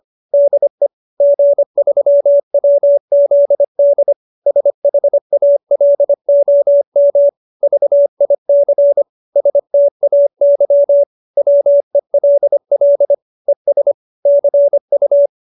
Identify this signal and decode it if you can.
Morse code, heard as a keyed tone, about 25 wpm, 570 Hz. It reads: DE G3WZD SHALOM VIC STAY WELL ES CU